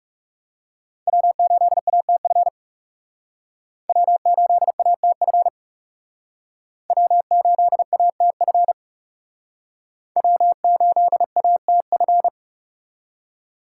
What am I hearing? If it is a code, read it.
Morse code, heard as a keyed tone, about 39 wpm, 700 Hz. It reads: W8ATF W8ATF W8ATF W8ATF